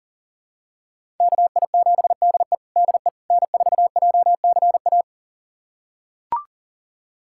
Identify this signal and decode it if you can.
Morse code, heard as a keyed tone, about 40 wpm, 700 Hz. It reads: KI7DE DE N4JCA
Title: USA Callsign DE Callsign - 5 Characters - 40wpm